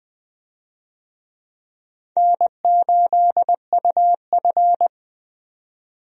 Morse code, heard as a keyed tone, about 20 wpm, 700 Hz. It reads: N8UF
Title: USA Call Signs - 4 Characters - 20wpm